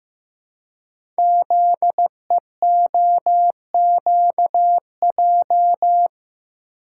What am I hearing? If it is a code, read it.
Morse code, heard as a keyed tone, about 15 wpm, 700 Hz. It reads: ZEOQJ
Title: Code Groups - 5 Letters - 15wpm